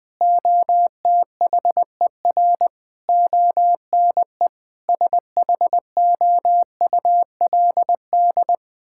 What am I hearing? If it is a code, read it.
Morse code, heard as a keyed tone, about 20 wpm, 700 Hz. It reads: OTHER ONE SHOULD
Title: Sentences from Top 1600 Words 20wpm